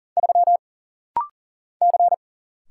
Morse code, heard as a keyed tone, about 40 wpm, 700 Hz. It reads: 3 C
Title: Single Letter-Number Alt - Rapid-Fire 40wpm